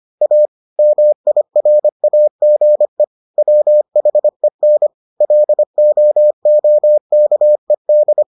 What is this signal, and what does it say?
Morse code, heard as a keyed tone, about 25 wpm, 600 Hz. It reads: A MIRAGE WHEN LOOKED